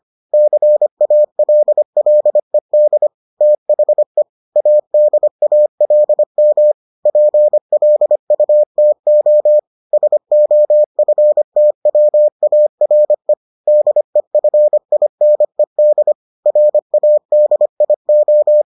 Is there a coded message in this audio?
Yes: Morse code from 0.3 to 18.6 s, heard as a keyed tone, about 25 wpm, 600 Hz. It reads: CALLED THE ADALM PLUTO SOFTWARE DEFINED RADIO